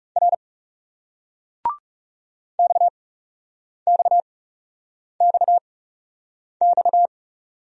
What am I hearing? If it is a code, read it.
Morse code, heard as a keyed tone, about 40 wpm, 700 Hz. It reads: R X X X X